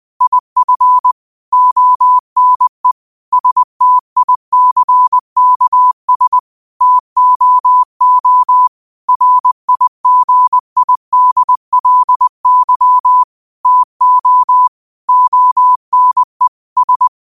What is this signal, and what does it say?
Morse code, heard as a keyed tone, about 20 wpm, 1000 Hz. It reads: IF ONE STICKS TOO RIGIDLY TO ONE S
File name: qotdfiles / 20wpm